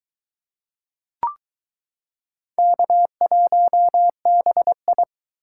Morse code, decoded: K1BI

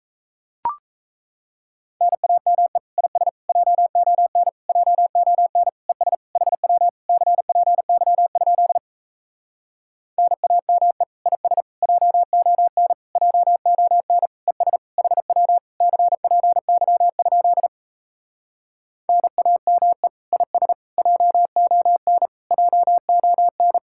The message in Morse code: NAME IS JON JON ES HW CPY? NAME IS JON JON ES HW CPY? NAME IS JON JON

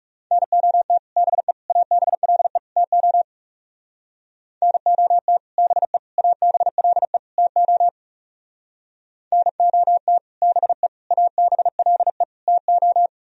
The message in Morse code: NOT BE ABLE TO NOT BE ABLE TO NOT BE ABLE TO

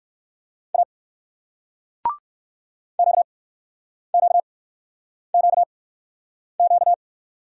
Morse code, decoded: A QQQQ

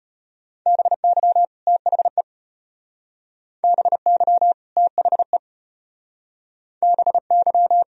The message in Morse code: BY THE BY THE BY

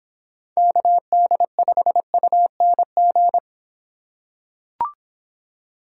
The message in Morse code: KD5UNG

